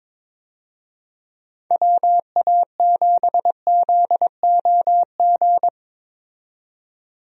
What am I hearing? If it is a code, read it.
Morse code, heard as a keyed tone, about 22 wpm, 700 Hz. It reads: WA7ZOG
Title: USA Call Signs - 6 Characters - 22wpm